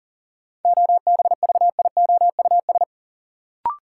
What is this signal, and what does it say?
Morse code, heard as a keyed tone, about 40 wpm, 700 Hz. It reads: OBVIOUS
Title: Sentences from Top 1600 Words - New Word Drill - 40wpm